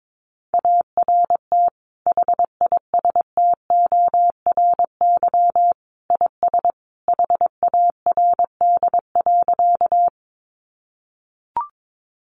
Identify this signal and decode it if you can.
Morse code, heard as a keyed tone, about 22 wpm, 700 Hz. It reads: ART HISTORY IS HARD.
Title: Sentences from Top 600 Words 22wpm